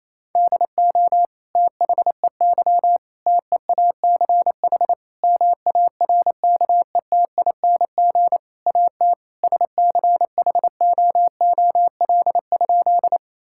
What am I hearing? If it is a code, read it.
Morse code, heard as a keyed tone, about 28 wpm, 700 Hz. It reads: DO THEY TEACH MARKETING AT SCHOOL?
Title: Sentences from Top 1000 Words - Review Part 5 - 28wpm